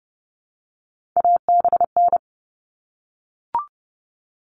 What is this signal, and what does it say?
Morse code, heard as a keyed tone, about 30 wpm, 700 Hz. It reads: ABN
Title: All Cut Numbers - 3-Digit 30wpm